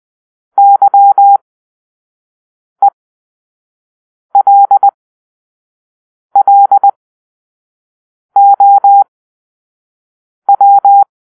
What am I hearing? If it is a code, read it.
Morse code, heard as a keyed tone, about 20 wpm, spaced as for 5 wpm, 800 Hz. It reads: YELLOW
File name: arrl / Grp-4 Drill-4